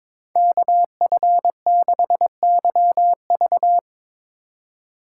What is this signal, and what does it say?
Morse code, heard as a keyed tone, about 22 wpm, 700 Hz. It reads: KF6YV
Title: USA Call Signs - 5 Characters - 3x Character Spacing - 22wpm